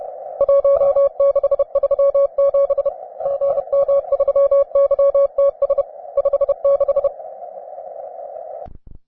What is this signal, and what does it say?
Morse code, heard as a keyed tone, about 31 wpm, 560 Hz. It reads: 1637 GM3YTS 56